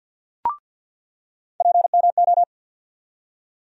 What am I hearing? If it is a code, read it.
Morse code, heard as a keyed tone, about 49 wpm, 700 Hz. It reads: PMO
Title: USA Call Sign Postfixes - 3 Letters 50wpm